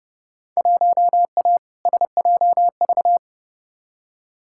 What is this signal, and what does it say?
Morse code, heard as a keyed tone, about 30 wpm, 700 Hz. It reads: 1A SJV
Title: Field Day Exchanges Encore - 30wpm